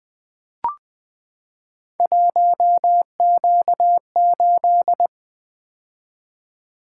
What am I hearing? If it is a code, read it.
Morse code, heard as a keyed tone, about 20 wpm, 700 Hz. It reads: 1Q8
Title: Three Letter-Number Combinations - 20wpm